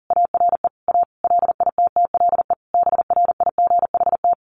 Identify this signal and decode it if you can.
Morse code, heard as a keyed tone, about 40 wpm, 700 Hz. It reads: ARE A LITTLE BRIGHT
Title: Sentences from Top 1000 Words 40wpm